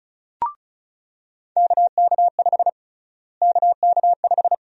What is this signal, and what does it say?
Morse code, heard as a keyed tone, about 36 wpm, 700 Hz. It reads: KK5 KK5
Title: USA Call Sign Prefixes - Repeated 2x - 35wpm